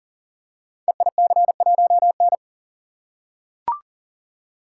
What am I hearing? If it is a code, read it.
Morse code, heard as a keyed tone, about 38 wpm, 700 Hz. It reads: EIC1N
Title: Code Groups - 5 Letter:Numbers - 40wpm